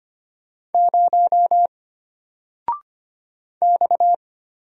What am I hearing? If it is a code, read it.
Morse code, heard as a keyed tone, about 25 wpm, 700 Hz. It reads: 0 X
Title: Single Letter-Number Alt 25wpm